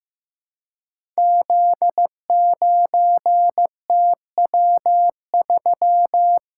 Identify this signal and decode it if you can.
Morse code, heard as a keyed tone, about 15 wpm, 700 Hz. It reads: Z9TW3
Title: Code Groups - 5 Letter:Numbers - 15wpm